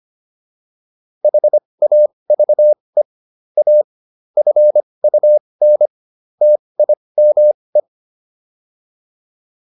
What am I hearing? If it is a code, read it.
Morse code, heard as a keyed tone, about 25 wpm, spaced as for 20 wpm, 600 Hz. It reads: HAVE A FUN TIME